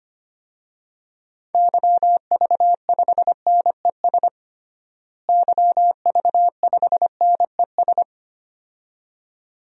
Morse code, heard as a keyed tone, about 25 wpm, 700 Hz. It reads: YV5NES YV5NES
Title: DXCC-100 Call Signs - 6 Characters - Repeated 2x - 25wpm